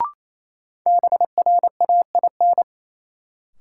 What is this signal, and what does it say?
Morse code, heard as a keyed tone, about 28 wpm, 700 Hz. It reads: BRAIN
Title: Sentences from Top 1100 Words - New Word Drill Alt - 28wpm